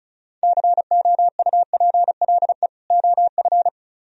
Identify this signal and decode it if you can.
Morse code, heard as a keyed tone, about 42 wpm, 700 Hz. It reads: COUPLE OF
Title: Speed-Racing - Sets of 2 Words 35wpm